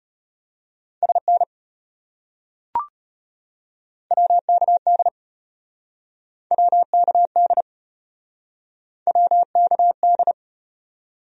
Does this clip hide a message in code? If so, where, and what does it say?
Yes: Morse code from 1.0 to 10.3 s, heard as a keyed tone, about 33 wpm, 700 Hz. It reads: SN WKD WKD WKD